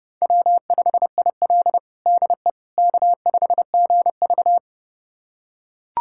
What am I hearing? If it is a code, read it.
Morse code, heard as a keyed tone, about 32 wpm, 700 Hz. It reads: W5IL DE K5GV
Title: USA Callsign DE Callsign - 4 Characters - 30wpm